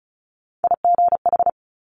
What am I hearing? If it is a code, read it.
Morse code, heard as a keyed tone, about 36 wpm, 700 Hz. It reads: IGH
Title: Top Three Letter Combinations in Words 35wpm